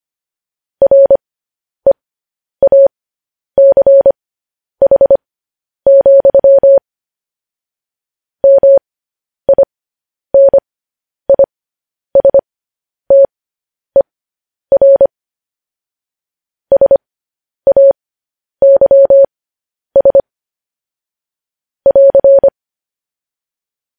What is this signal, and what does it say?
Morse code, heard as a keyed tone, about 25 wpm, spaced as for 10 wpm, 550 Hz. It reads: REACH, MINISTER SAYS <AR>
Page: news headlines in morse code at 10 wpm